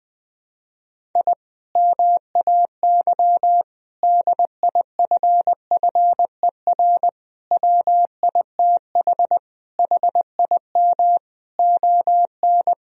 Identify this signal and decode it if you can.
Morse code, heard as a keyed tone, about 20 wpm, 700 Hz. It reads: I MAY DIFFER WITH HIM ON
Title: Sentences from Top 1000 Words - Review Part 1 - 20wpm